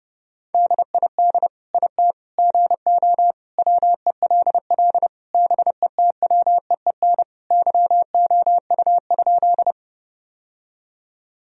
Morse code, heard as a keyed tone, about 30 wpm, 700 Hz. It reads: DID IT GO WELL BETWEEN YOU?
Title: Sentences from Top 1000 Words - Review Part 1 - 30wpm